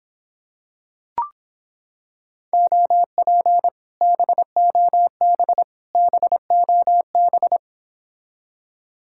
Morse code, heard as a keyed tone, about 26 wpm, 700 Hz. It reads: OP BOB BOB